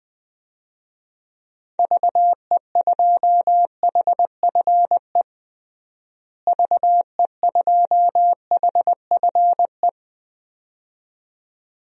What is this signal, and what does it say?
Morse code, heard as a keyed tone, about 20 wpm, 700 Hz. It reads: VE2HFE VE2HFE
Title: DXCC-100 Call Signs - 6 Characters - Repeated 2x - 20wpm